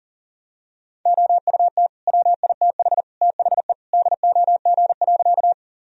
Morse code, heard as a keyed tone, about 40 wpm, 700 Hz. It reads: OUT WITH THE DOG.
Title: Sentences from Top 1000 Words - Review Part 2 - 40wpm